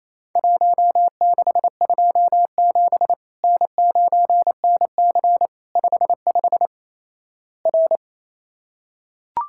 Morse code, heard as a keyed tone, about 27 wpm, 700 Hz. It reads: 1627 N9NC 55 R
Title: Worked All Europe DX Contest - QTC Exchange - 28wpm